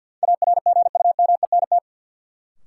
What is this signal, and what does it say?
Morse code, heard as a keyed tone, about 53 wpm, 700 Hz. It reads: ARGUMENT
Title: Sentences from Top 1400 Words - New Word Drill - 50wpm